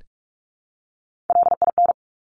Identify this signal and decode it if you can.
Morse code, heard as a keyed tone, about 46 wpm, 700 Hz. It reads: LIN
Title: Top Three Letter Combinations in Words 45wpm